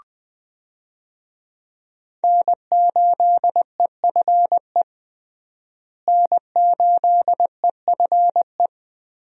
Morse code, heard as a keyed tone, about 20 wpm, 700 Hz. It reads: N8EFE N8EFE
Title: USA Call Signs - 5 Characters - Repeated 3x - 20wpm